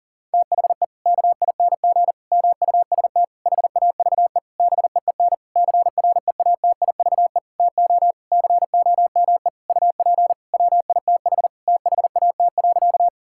Morse code, heard as a keyed tone, about 40 wpm, 700 Hz. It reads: THE KING MUST HAVE BEEN CREATIVE TO COME UP WITH THAT.
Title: Sentences from Top 1000 Words - Review Part 5 - 40wpm